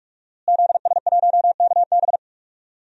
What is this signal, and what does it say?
Morse code, heard as a keyed tone, about 41 wpm, 700 Hz. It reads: ZS1KB
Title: DXCC-100 Call Signs - 5 Characters - Repeated 2x - 45wpm